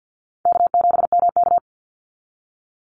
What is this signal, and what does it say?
Morse code, heard as a keyed tone, about 50 wpm, 700 Hz. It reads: K7MK
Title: NA SOTA Activator Callsigns 50wpm